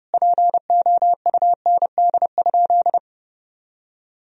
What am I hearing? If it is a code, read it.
Morse code, heard as a keyed tone, about 30 wpm, 700 Hz. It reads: POUND?
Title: Sentences from Top 1000 Words - Review Part 2 - 30wpm